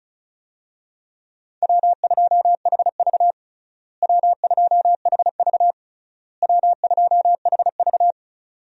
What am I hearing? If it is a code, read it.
Morse code, heard as a keyed tone, about 34 wpm, 700 Hz. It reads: W2HV W2HV W2HV
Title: USA Call Signs - 4 Characters - Repeated 3x - 35wpm